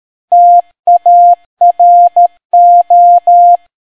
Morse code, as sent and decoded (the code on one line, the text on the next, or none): - .- .-. ---
TARO